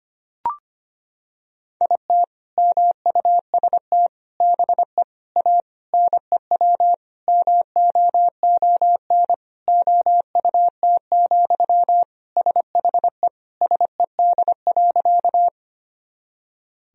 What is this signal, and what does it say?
Morse code, heard as a keyed tone, about 25 wpm, 700 Hz. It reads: IT MUST BE A NEW MOON OUT, SHE SED.
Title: Sentences from Top 500 Words and Abbreviations 25wpm